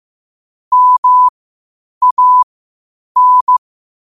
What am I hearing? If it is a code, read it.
Morse code, heard as a keyed tone, about 15 wpm, spaced as for 8 wpm, 1000 Hz. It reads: MAN
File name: qotdfiles / 8wpm